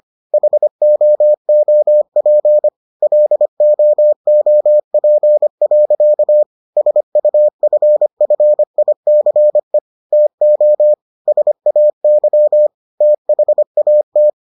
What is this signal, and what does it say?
Morse code, heard as a keyed tone, about 25 wpm, 600 Hz. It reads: HOOP LOOP. SUFFICE TO SAY THAT